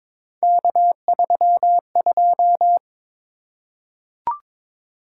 Morse code, heard as a keyed tone, about 22 wpm, 700 Hz. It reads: K32
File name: practice / 22wpm / Three Letter-Number Combinations - 22wpm